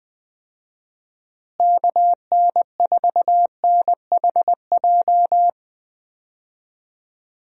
Morse code, heard as a keyed tone, about 20 wpm, 700 Hz. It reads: KN4NHJ
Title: USA Call Signs - 6 Characters - Repeated 2x - 20wpm